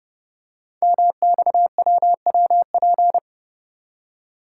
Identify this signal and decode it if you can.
Morse code, heard as a keyed tone, about 30 wpm, 700 Hz. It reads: MXWWP
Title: Code Groups - 5 Letters - 30wpm